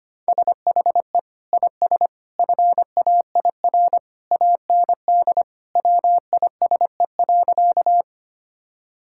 Morse code, heard as a keyed tone, about 25 wpm, 700 Hz. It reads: SHE IS FAIR AND WISE.
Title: Sentences from Top 1600 Words 25wpm